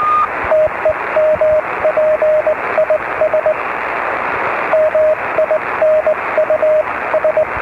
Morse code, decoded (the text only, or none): TEMP IS MINUS